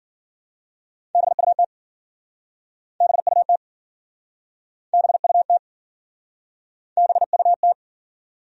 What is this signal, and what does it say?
Morse code, heard as a keyed tone, about 53 wpm, 700 Hz. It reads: BUT BUT BUT BUT